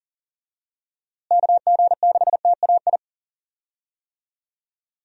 Morse code, heard as a keyed tone, about 40 wpm, 700 Hz. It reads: KG6TAI
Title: USA Call Signs - 6 Characters - Repeated 2x - 40wpm